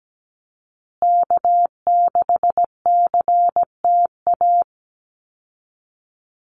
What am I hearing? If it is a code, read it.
Morse code, heard as a keyed tone, about 17 wpm, 700 Hz. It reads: K6CTA
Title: Call Signs Repeated 2x - 17wpm